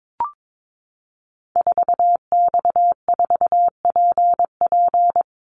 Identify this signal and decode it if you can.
Morse code, heard as a keyed tone, about 22 wpm, 700 Hz. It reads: 4X4PP
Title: Call Signs - 22wpm